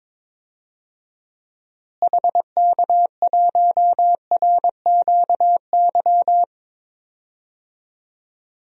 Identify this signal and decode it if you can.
Morse code, heard as a keyed tone, about 22 wpm, 700 Hz. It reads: HK1RQY
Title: DXCC-100 Call Signs - 6 Characters - Repeated 2x - 22wpm